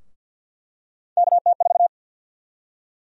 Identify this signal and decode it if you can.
Morse code, heard as a keyed tone, about 46 wpm, 700 Hz. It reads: KT4
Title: USA Call Sign Prefixes - 50wpm